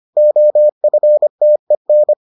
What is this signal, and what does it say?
Morse code, heard as a keyed tone, about 25 wpm, 600 Hz. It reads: OFTEN